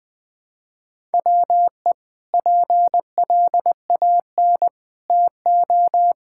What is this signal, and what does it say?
Morse code, heard as a keyed tone, about 20 wpm, 700 Hz. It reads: WE PLAN TO
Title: Sentences with 6-Word Phrases - 20wpm